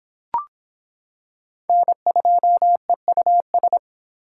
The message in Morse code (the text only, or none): N2EUS